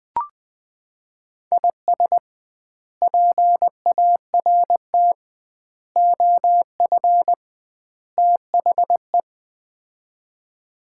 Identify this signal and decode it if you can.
Morse code, heard as a keyed tone, about 20 wpm, 700 Hz. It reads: IS PART OF THE
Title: 2x Word-Spacing - Sets of 4 Words - 20wpm